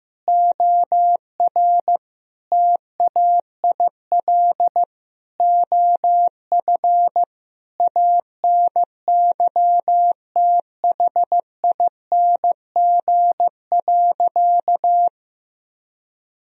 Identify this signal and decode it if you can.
Morse code, heard as a keyed tone, about 15 wpm, 700 Hz. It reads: OR TAIL OF ANYTHING.
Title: Sentences from Top 1000 Words - Review Part 3 - 15wpm